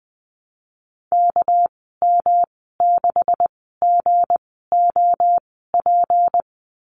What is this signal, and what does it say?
Morse code, heard as a keyed tone, about 19 wpm, spaced as for 14 wpm, 700 Hz. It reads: KM6GOP
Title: 207 Lesson 33g - Letter j - Call Signs 20wpm